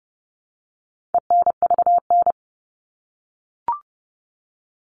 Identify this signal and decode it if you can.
Morse code, heard as a keyed tone, about 30 wpm, 700 Hz. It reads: ENVN
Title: All Cut Numbers - 4-Digit 30wpm